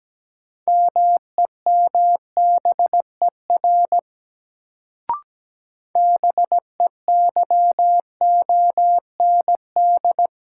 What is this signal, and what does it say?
Morse code, heard as a keyed tone, about 17 wpm, 700 Hz. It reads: MEMBER BEYOND
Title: Sentences from Top 1100 Words - New Word Drill Alt - 17wpm